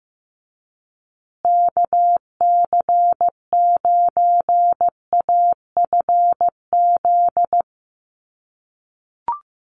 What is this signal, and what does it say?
Morse code, heard as a keyed tone, about 15 wpm, 700 Hz. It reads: KC9AFZ
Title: USA Call Signs Repeated 3x 15wpm